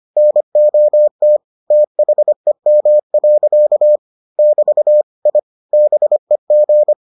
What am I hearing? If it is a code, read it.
Morse code, heard as a keyed tone, about 25 wpm, 600 Hz. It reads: NOT THEM. <BT> I BEG